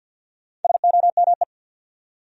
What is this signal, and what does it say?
Morse code, heard as a keyed tone, about 47 wpm, 700 Hz. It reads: SOME
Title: Top 100 Words 50wpm